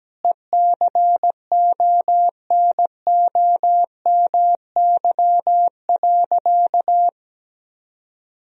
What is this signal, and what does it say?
Morse code, heard as a keyed tone, about 17 wpm, 700 Hz. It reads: ECONOMY.
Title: Sentences from Top 1000 Words - Review Part 3 - 17wpm